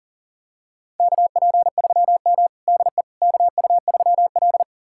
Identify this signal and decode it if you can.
Morse code, heard as a keyed tone, about 37 wpm, 700 Hz. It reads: KP3M DE KU3L
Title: USA Callsign DE Callsign - 4 Characters - 40wpm